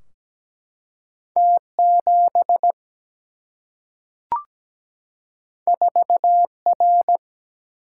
Morse code, heard as a keyed tone, about 17 wpm, 700 Hz. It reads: T7 4R
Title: Single-Letter Single-Number 17wpm